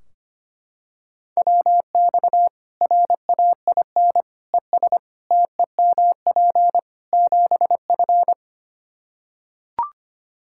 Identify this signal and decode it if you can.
Morse code, heard as a keyed tone, about 25 wpm, 700 Hz. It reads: WX RAIN ES TEMP 7F
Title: Practice QSO - Exchange 2 - Info - 25wpm